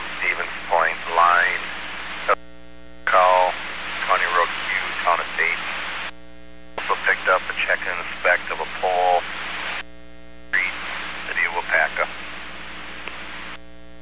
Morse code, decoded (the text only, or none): none